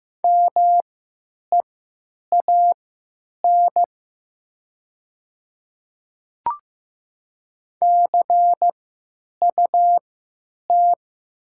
Morse code, heard as a keyed tone, about 15 wpm, spaced as for 9 wpm, 700 Hz. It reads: MEAN CUT
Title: 3x Character Spacing - Top 300 Words - 15wpm